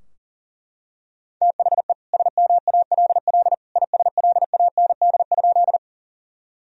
Morse code, heard as a keyed tone, about 40 wpm, 700 Hz. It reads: THE SMALL ISLAND?
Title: Sentences from Top 1000 Words - Review Part 3 - 40wpm